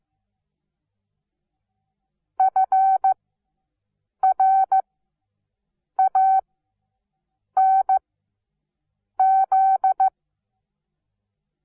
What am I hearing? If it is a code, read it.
Morse code, heard as a keyed tone, about 15 wpm, spaced as for 6 wpm, 770 Hz. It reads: FRANZ
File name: Onlinelehrgang / morsen / DJ4UF Morsen17